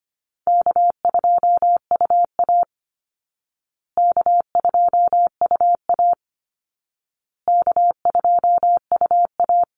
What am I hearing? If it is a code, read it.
Morse code, heard as a keyed tone, about 25 wpm, 700 Hz. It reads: K2UA K2UA K2UA
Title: Call Signs Repeated 2x - 25wpm